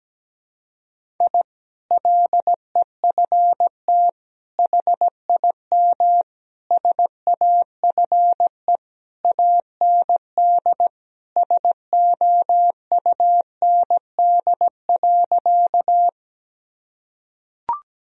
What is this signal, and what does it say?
Morse code, heard as a keyed tone, about 17 wpm, 700 Hz. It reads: I LEFT HIM SAFE AND SOUND.
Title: Sentences from Top 1000 Words - Review Part 1 - 17wpm